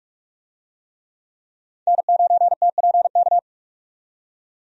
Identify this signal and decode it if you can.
Morse code, heard as a keyed tone, about 46 wpm, 700 Hz. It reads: N9TPK
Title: USA Call Signs - 5 Characters - 45wpm